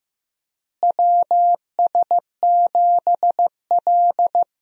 Morse code, heard as a keyed tone, about 16 wpm, 700 Hz. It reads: WS7L
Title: USA Call Signs - 4 Characters - Repeated 2x - 15wpm